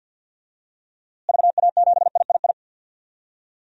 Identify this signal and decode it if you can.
Morse code, heard as a keyed tone, about 46 wpm, 700 Hz. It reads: VA7III
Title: DXCC-100 Call Signs - 6 Characters - Repeated 3x - 50wpm